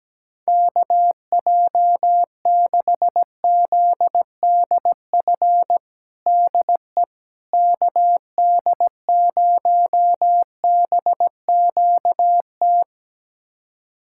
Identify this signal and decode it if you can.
Morse code, heard as a keyed tone, about 17 wpm, 700 Hz. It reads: KJ6ZDF DE KD0BQT
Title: USA Callsign DE Callsign - 6 Characters - 17wpm